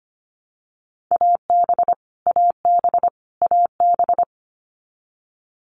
Morse code, heard as a keyed tone, about 25 wpm, 700 Hz. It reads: AB AB AB
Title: Canadian Provinces and Territories - Repeated 3x - 25wpm